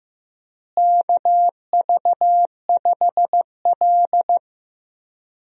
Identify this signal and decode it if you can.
Morse code, heard as a keyed tone, about 15 wpm, 700 Hz. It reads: KV5L
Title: USA Call Signs - 4 Characters - Repeated 3x - 15wpm